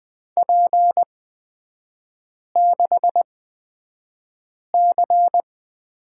Morse code, heard as a keyed tone, about 20 wpm, spaced as for 5 wpm, 700 Hz. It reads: P6C
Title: 224 Lesson 36e - BT - Three Characters 20wpm